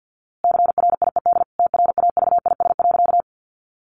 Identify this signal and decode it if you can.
Morse code, heard as a keyed tone, about 50 wpm, 700 Hz. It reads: CRIED TRAVIS.
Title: A Sound Of Thunder 50wpm